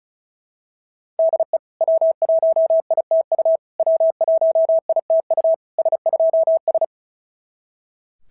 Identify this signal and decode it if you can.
Morse code, heard as a keyed tone, about 36 wpm, 690 Hz. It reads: DE W1ITU W1ITU S2S
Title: SOTA QSO Practice 35wpm